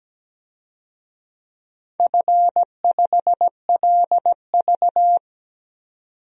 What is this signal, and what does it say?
Morse code, heard as a keyed tone, about 17 wpm, 700 Hz. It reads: F5LV
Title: DXCC-100 Call Signs - 4 Characters - 17wpm